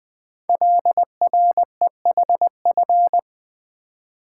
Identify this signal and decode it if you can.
Morse code, heard as a keyed tone, about 20 wpm, 700 Hz. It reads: LREHF
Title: Code Groups - 5 Letters - 20wpm